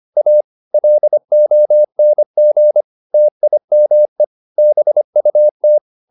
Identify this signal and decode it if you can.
Morse code, heard as a keyed tone, about 25 wpm, 600 Hz. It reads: A LONG TIME BUT